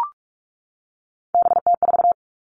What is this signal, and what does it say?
Morse code, heard as a keyed tone, about 44 wpm, 700 Hz. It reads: BT4
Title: All Cut Numbers - 3-Digit 45wpm